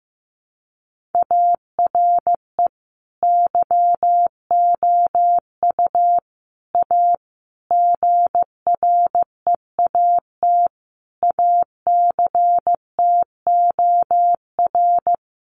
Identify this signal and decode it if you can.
Morse code, heard as a keyed tone, about 14 wpm, 700 Hz. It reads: ARE YOU A GREAT ACTOR
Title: Sentences from Top 1500 Words 15wpm